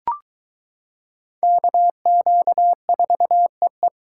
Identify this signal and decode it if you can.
Morse code, heard as a keyed tone, about 22 wpm, 700 Hz. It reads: KQ4EE